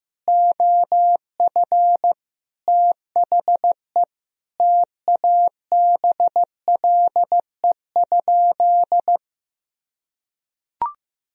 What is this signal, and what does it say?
Morse code, heard as a keyed tone, about 15 wpm, 700 Hz. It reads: OF THE TABLE?
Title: Sentences from Top 1000 Words - Review Part 3 - 15wpm